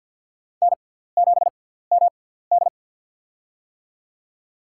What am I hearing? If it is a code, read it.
Morse code, heard as a keyed tone, about 49 wpm, spaced as for 17 wpm, 700 Hz. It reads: N7MD
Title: USA Call Signs - 4 Characters - 6x Character Spacing - 50wpm